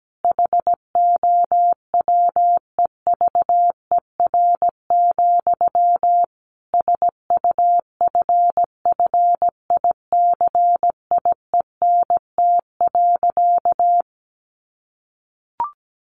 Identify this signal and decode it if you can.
Morse code, heard as a keyed tone, about 17 wpm, 700 Hz. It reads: HOWEVER, SUFFICIENT.
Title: Sentences from Top 1400 Words 17wpm